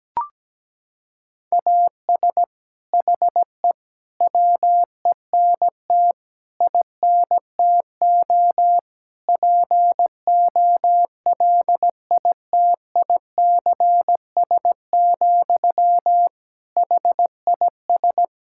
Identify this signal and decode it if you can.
Morse code, heard as a keyed tone, about 17 wpm, 700 Hz. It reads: AS HE WENT INTO POLITICS, HIS
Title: Sentences from Top 1600 Words 17wpm